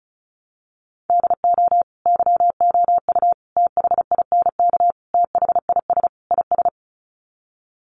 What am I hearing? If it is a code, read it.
Morse code, heard as a keyed tone, about 34 wpm, 700 Hz. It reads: DO YOU THINK THIS IS
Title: Sets of 5 Words Encore2 35wpm